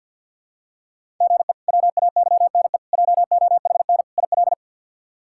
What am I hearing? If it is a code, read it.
Morse code, heard as a keyed tone, about 46 wpm, 700 Hz. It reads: GE WAYNE JOHN IL
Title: K1USN SST QSO Practice - 50wpm